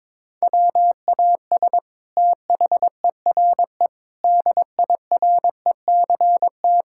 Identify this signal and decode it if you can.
Morse code, heard as a keyed tone, about 21 wpm, 700 Hz. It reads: WAS THERE DIRECT
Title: Sentences from Top 1000 Words - Review Part 6 - 22wpm